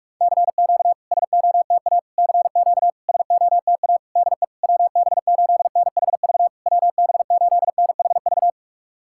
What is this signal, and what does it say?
Morse code, heard as a keyed tone, about 46 wpm, 690 Hz. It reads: CQ SOTA CQ SOTA DE WB8NHV WB8NHV
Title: SOTA QSO Practice 45wpm